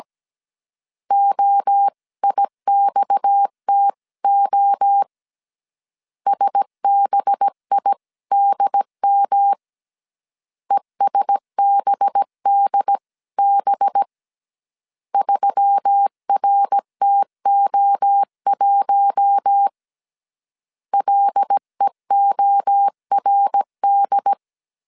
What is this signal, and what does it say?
Morse code, heard as a keyed tone, about 17 wpm, 800 Hz. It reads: OIXTO SBIDM ESBDB 3RTO1 LEORD
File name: 16wpm